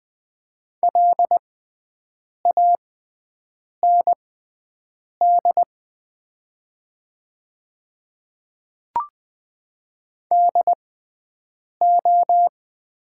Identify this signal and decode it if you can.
Morse code, heard as a keyed tone, about 20 wpm, spaced as for 7 wpm, 700 Hz. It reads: LAND DO
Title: 6x Character Spacing - Top 200 Words - 20wpm